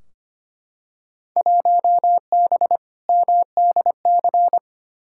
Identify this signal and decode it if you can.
Morse code, heard as a keyed tone, about 25 wpm, 700 Hz. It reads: 1B MDC
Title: Field Day Exchanges - 25wpm